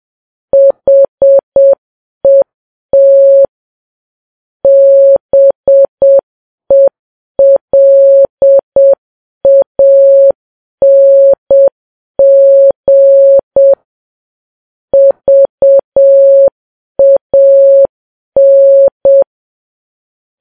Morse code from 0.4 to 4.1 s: HET